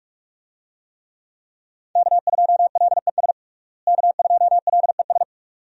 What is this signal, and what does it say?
Morse code, heard as a keyed tone, about 45 wpm, 700 Hz. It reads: K2LES K2LES
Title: USA Call Signs - 5 Characters - Repeated 2x - 45wpm